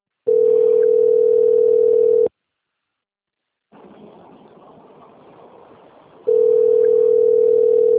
Ringback tone (North America), 440 + 480 Hz.
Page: Limeira (Frente)